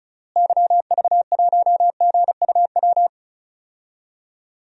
Morse code, heard as a keyed tone, about 36 wpm, 700 Hz. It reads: YV1GUW
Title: DXCC-100 Call Signs - 6 Characters - Repeated 3x - 35wpm